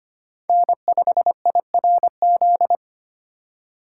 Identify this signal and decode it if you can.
Morse code, heard as a keyed tone, about 25 wpm, 700 Hz. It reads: N5IRZ
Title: USA Call Signs - 5 Characters - 8x Character Spacing - 25wpm